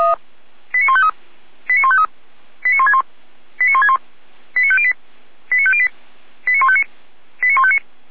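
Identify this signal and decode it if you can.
Five-tone selective call (ZVEI), heard as quick runs of five tones, tones of about 70 ms. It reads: 89153 89153 89161 89161 89508 89508 89159 89159
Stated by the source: ZVEI five-tone selective call 89165 received by SDR.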